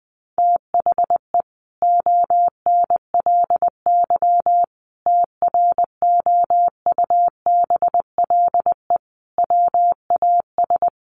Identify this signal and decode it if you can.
Morse code, heard as a keyed tone, about 20 wpm, 700 Hz. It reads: THE ONLY TROUBLE WAS
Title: Sentences from Top 1200 Words 20wpm